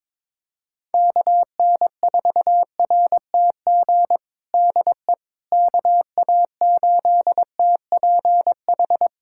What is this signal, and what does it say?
Morse code, heard as a keyed tone, about 22 wpm, 700 Hz. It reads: KN4RTG DE KA8TPH
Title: USA Callsign DE Callsign - 6 Characters - 22wpm